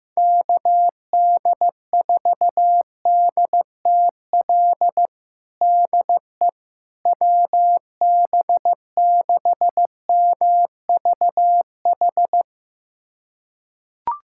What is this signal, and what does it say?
Morse code, heard as a keyed tone, about 15 wpm, 700 Hz. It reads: KD4DTL DE WB6MVH
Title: USA Callsign DE Callsign - 6 Characters - 15wpm